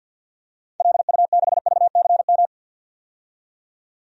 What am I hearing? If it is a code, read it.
Morse code, heard as a keyed tone, about 51 wpm, 700 Hz. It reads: LU6VCM
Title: DXCC-100 Call Signs - 6 Characters - 50wpm